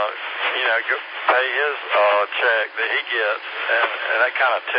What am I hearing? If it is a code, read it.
Emergency Alert System (SAME) tones, heard as harsh buzzing bursts, with speech. It is heard from 0.6 to 4.3 s.